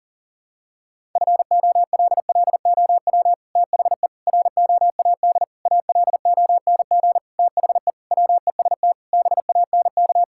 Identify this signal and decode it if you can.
Morse code, heard as a keyed tone, about 40 wpm, 700 Hz. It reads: FOLLOW THE ROAD ALONG THE WEST BANK
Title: Sentences from Top 1000 Words - Review Part 6 - 40wpm